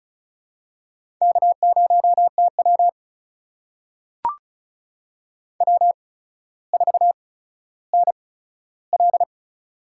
Morse code, heard as a keyed tone, about 34 wpm, spaced as for 15 wpm, 700 Hz. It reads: K0TW W4NL
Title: USA Call Signs - 4 Characters - 8x Character Spacing - 35wpm